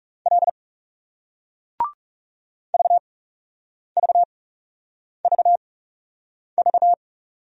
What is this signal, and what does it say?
Morse code, heard as a keyed tone, about 40 wpm, spaced as for 8 wpm, 700 Hz. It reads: L VVVV